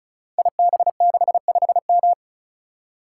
Morse code, heard as a keyed tone, about 36 wpm, 700 Hz. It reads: IB65M
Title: Code Groups - 5 Letter:Numbers - 35wpm